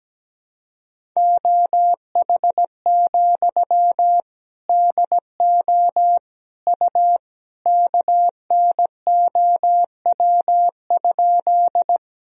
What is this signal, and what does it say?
Morse code, heard as a keyed tone, about 17 wpm, 700 Hz. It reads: OH, DO U KNOW?
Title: Sentences from Top 500 Words and Abbreviations 17wpm